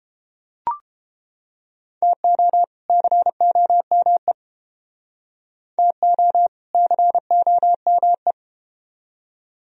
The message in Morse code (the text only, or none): TO COME TO COME